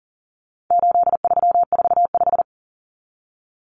Morse code, heard as a keyed tone, about 40 wpm, 700 Hz. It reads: 8345
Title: Common Cut Numbers - 4-Digit 40wpm